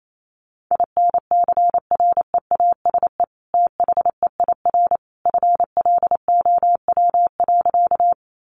Morse code, heard as a keyed tone, about 28 wpm, 700 Hz. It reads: INCREASE THEIR FLOW.
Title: Sentences from Top 1200 Words 28wpm